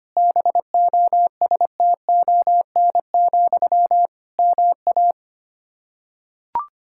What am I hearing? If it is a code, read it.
Morse code, heard as a keyed tone, about 25 wpm, 700 Hz. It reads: BOSTON, MA
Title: Practice QSO - Exchange 1 - Beginning - 25wpm